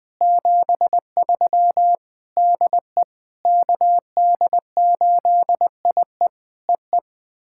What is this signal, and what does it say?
Morse code, heard as a keyed tone, about 20 wpm, 700 Hz. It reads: 73 DE KD8IE EE
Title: POTA QSO Practice - 20wpm